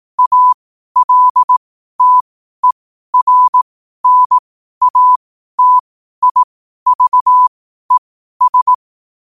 Morse code, heard as a keyed tone, about 18 wpm, spaced as for 12 wpm, 1000 Hz. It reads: ALTERNATIVES
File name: qotdfiles / 12wpm